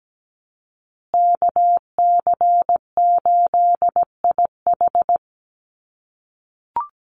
Morse code, heard as a keyed tone, about 17 wpm, 700 Hz. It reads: KC8IH
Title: USA Call Signs Repeated 2x 17wpm